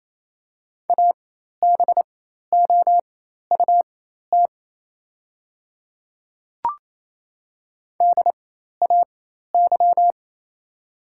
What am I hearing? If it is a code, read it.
Morse code, heard as a keyed tone, about 28 wpm, spaced as for 13 wpm, 700 Hz. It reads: ABOUT DAY
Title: 4x Character Spacing - Top 200 Words - 28wpm